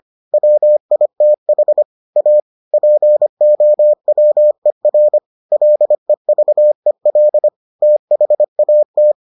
Morse code, heard as a keyed tone, about 25 wpm, 600 Hz. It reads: WITH A POWER LEVEL THAT